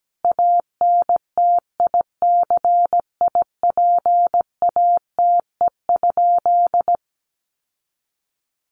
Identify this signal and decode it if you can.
Morse code, heard as a keyed tone, about 17 wpm, 700 Hz. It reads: ANTICIPATE?
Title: Sentences from Top 1500 Words 17wpm